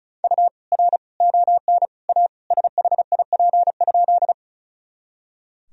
Morse code, heard as a keyed tone, about 36 wpm, 700 Hz. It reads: U R ON A SHIP?
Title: Sentences from Top 400 Words and Abbreviations 35wpm